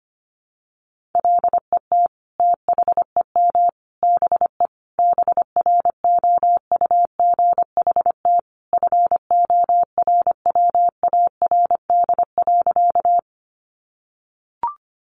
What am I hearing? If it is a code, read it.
Morse code, heard as a keyed tone, about 25 wpm, 700 Hz. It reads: LET THEM BE BROUGHT FORWARD.
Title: Sentences from Top 1300 Words 25wpm